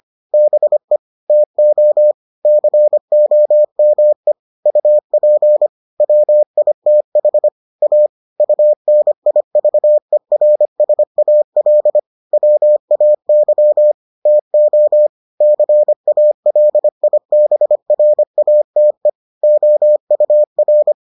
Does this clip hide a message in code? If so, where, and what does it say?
Yes: Morse code from 0.3 to 20.9 s, heard as a keyed tone, about 25 wpm, 600 Hz. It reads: BE TO COME UP WITH A UNIVERSAL WAY TO CALIBRATE OUR